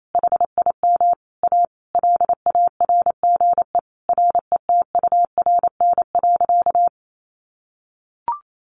Morse code, heard as a keyed tone, about 28 wpm, 700 Hz. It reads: HIM A LARGE RETURN.
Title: Sentences from Top 1100 Words 28wpm